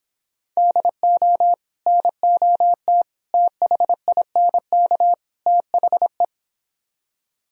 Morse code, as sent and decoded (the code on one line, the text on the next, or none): -.. --- / -. --- - / - .... .. -. -.- / - .... .
DO NOT THINK THE